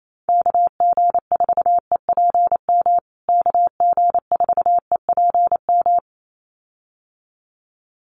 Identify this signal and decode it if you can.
Morse code, heard as a keyed tone, about 28 wpm, 700 Hz. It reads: KG4EPM KG4EPM
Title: USA Call Signs Repeated 2x 28wpm